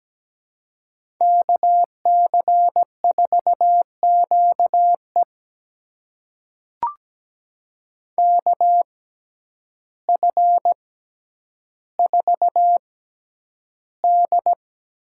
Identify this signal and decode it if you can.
Morse code, heard as a keyed tone, about 17 wpm, spaced as for 9 wpm, 700 Hz. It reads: KC4QE KF4D
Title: USA Call Signs - 6x Character Spacing - 17wpm